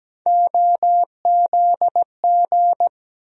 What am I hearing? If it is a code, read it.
Morse code, heard as a keyed tone, about 17 wpm, 700 Hz. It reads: OZG
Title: USA Call Sign Postfixes - 3 Letters 17wpm